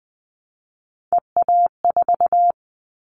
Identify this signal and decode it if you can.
Morse code, heard as a keyed tone, about 20 wpm, 700 Hz. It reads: EA4
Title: Callsign Prefixes for Top 25 DXCC - Repeated 3x - 20wpm